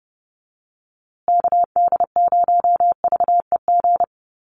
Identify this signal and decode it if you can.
Morse code, heard as a keyed tone, about 30 wpm, 700 Hz. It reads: KD0VEG
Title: USA Call Signs Repeated 2x 30wpm